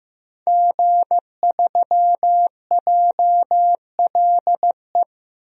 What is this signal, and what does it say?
Morse code, heard as a keyed tone, about 15 wpm, 700 Hz. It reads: G3JLE
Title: DXCC-100 Call Signs - 5 Characters - 15wpm